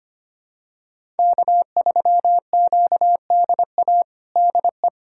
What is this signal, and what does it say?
Morse code, heard as a keyed tone, about 25 wpm, 700 Hz. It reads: K3QDA DE
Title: USA Callsign DE Callsign - 5 Characters - 25wpm